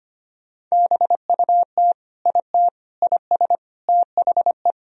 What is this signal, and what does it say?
Morse code, heard as a keyed tone, about 25 wpm, 700 Hz. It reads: BUT IT IS THE
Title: Sentences from Top 300 Words and Abbreviations 25wpm